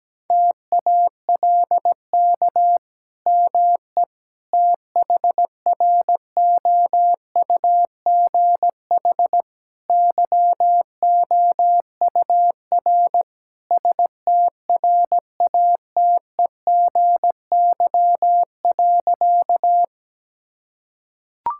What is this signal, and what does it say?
Morse code, heard as a keyed tone, about 17 wpm, 700 Hz. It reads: TALK ME THROUGH YOUR STRATEGY.
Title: Sentences from Top 1000 Words - Review Part 5 - 17wpm